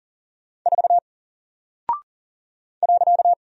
Morse code, heard as a keyed tone, about 40 wpm, 700 Hz. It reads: 4 .
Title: Single Character (Letter:Number:Punctuation:Prosign) 40wpm